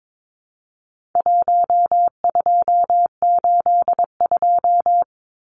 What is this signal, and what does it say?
Morse code, heard as a keyed tone, about 22 wpm, 700 Hz. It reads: 1282
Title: Numbers - 4-Digit 22wpm